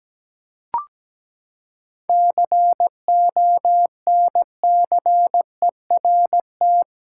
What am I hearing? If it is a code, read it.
Morse code, heard as a keyed tone, about 17 wpm, 700 Hz. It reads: CONCERT
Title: Seven Letter Words - 17wpm